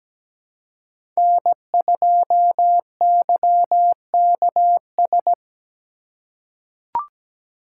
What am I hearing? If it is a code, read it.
Morse code, heard as a keyed tone, about 17 wpm, 700 Hz. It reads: N2YKS
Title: USA Call Signs - 5 Characters - 4x Character Spacing - 17wpm